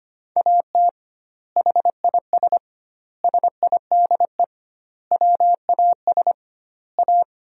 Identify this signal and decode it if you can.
Morse code, heard as a keyed tone, about 25 wpm, 700 Hz. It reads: AT HIS SIDE WAS A
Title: 2x Word-Spacing - Sentences from Top 500 Words - 25wpm